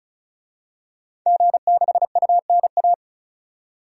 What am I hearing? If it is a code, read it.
Morse code, heard as a keyed tone, about 36 wpm, 700 Hz. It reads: G6UNA
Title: DXCC-100 Call Signs - 5 Characters - 35wpm